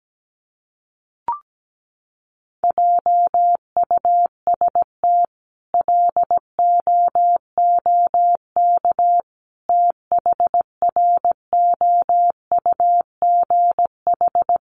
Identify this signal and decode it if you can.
Morse code, heard as a keyed tone, about 17 wpm, 700 Hz. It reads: JUST LOOK THROUGH
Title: Sentences from Top 200 Words 17wpm